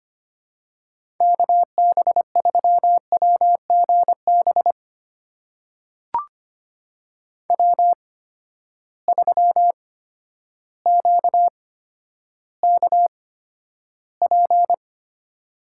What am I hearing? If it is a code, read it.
Morse code, heard as a keyed tone, about 25 wpm, spaced as for 11 wpm, 700 Hz. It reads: KB3WGB W3QKP
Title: USA Call Signs - 8x Character Spacing - 25wpm